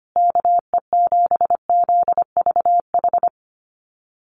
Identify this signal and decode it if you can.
Morse code, heard as a keyed tone, about 25 wpm, 700 Hz. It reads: KE7ZVH
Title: USA Call Signs Repeated 3x 25wpm